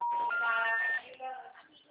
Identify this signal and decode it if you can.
Special information tone, with speech.